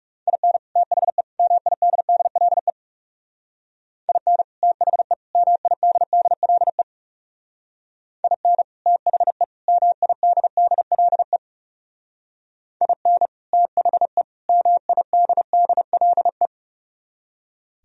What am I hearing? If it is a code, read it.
Morse code, heard as a keyed tone, about 36 wpm, 700 Hz. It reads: IN THE MIDDLE IN THE MIDDLE IN THE MIDDLE IN THE MIDDLE
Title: Speed-Racing - Sets of 3 Words 30wpm